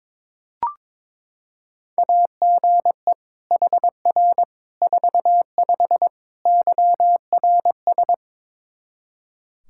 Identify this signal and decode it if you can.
Morse code, heard as a keyed tone, about 22 wpm, 700 Hz. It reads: AGE HR 45 YRS
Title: K1USN SST QSO Practice 22wpm